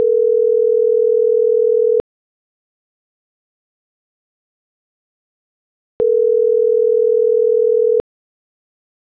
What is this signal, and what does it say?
Ringback tone (North America), 440 + 480 Hz.